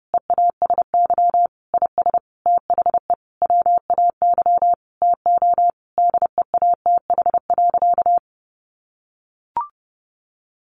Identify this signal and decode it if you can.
Morse code, heard as a keyed tone, about 28 wpm, 700 Hz. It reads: EASY IS THE WAY TO DEATH.
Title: Sentences from Top 1400 Words 30wpm